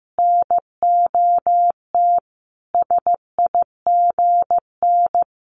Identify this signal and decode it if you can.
Morse code, heard as a keyed tone, about 15 wpm, 700 Hz. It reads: NOT SIGN
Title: Sentences from Top 1000 Words 15wpm